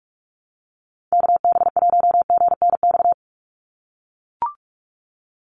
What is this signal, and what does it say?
Morse code, heard as a keyed tone, about 44 wpm, 700 Hz. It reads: KB1GNX
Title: USA Call Signs 45wpm